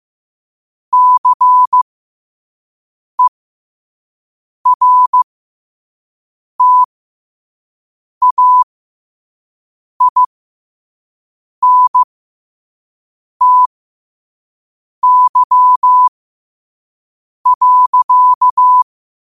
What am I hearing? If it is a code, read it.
Morse code, heard as a keyed tone, about 15 wpm, spaced as for 5 wpm, 1000 Hz. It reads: CERTAINTY.
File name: qotdfiles / 5wpm